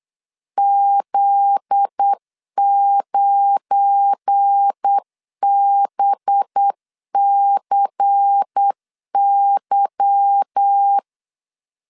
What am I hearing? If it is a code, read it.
Morse code, heard as a keyed tone, about 8 wpm, 800 Hz. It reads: Z9BCY
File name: LEZIONI / 9 28 8wpm 3512